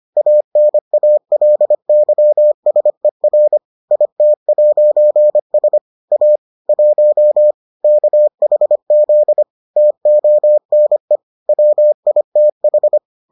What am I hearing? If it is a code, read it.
Morse code, heard as a keyed tone, about 25 wpm, 600 Hz. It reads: ANALYSER IT'S A 1 KHZ TONE WITH